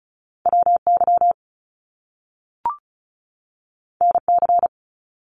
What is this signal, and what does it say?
Morse code, heard as a keyed tone, about 36 wpm, 700 Hz. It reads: WY NC
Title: US States 35wpm